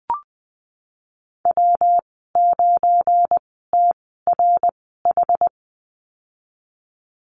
Morse code, heard as a keyed tone, about 20 wpm, spaced as for 15 wpm, 700 Hz. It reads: W9TRH
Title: 213 Lesson 34g - Number 8 - Call Signs 20wpm